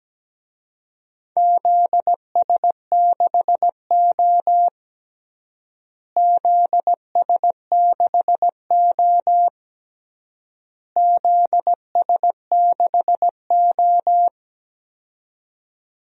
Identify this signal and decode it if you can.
Morse code, heard as a keyed tone, about 17 wpm, 700 Hz. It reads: ZS6O ZS6O ZS6O
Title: DXCC-100 Call Signs - 4 Characters - Repeated 3x - 17wpm